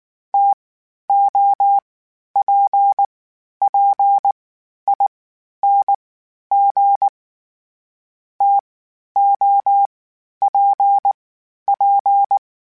Morse code, heard as a keyed tone, about 19 wpm, spaced as for 11 wpm, 800 Hz. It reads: TOPPING TOPP